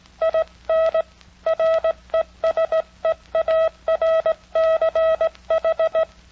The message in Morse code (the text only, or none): IN RESEARCH